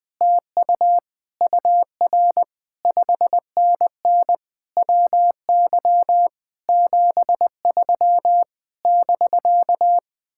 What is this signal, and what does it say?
Morse code, heard as a keyed tone, about 20 wpm, 700 Hz. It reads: TU UR 5NN WY 73 <BK>
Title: Speed-Racing Common QSO Phrases - 20wpm